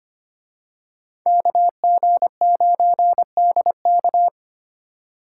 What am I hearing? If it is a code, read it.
Morse code, heard as a keyed tone, about 29 wpm, 700 Hz. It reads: KG9DK
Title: Speed Racing - USA 2x2 Call Signs 25wpm